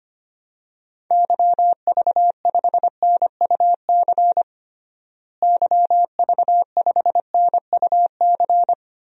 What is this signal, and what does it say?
Morse code, heard as a keyed tone, about 25 wpm, 700 Hz. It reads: YV5NUC YV5NUC
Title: DXCC-100 Call Signs - 6 Characters - Repeated 2x - 25wpm